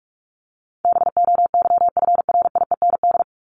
Morse code, heard as a keyed tone, about 44 wpm, 700 Hz. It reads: BOYFRIEND
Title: Top 501-750 Words 45wpm